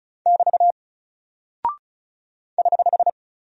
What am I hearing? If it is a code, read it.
Morse code, heard as a keyed tone, about 35 wpm, 700 Hz. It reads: <BT> <HH>
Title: Single Character (Letter:Number:Punctuation:Prosign) 35wpm